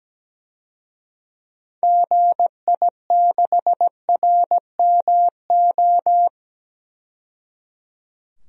Morse code, heard as a keyed tone, about 17 wpm, 700 Hz. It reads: GI6RMO
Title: DXCC-100 Call Signs - 6 Characters - 17wpm